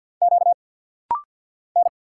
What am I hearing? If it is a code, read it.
Morse code, heard as a keyed tone, about 50 wpm, 700 Hz. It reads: Q N
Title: Single Letters - Rapid-Fire 50wpm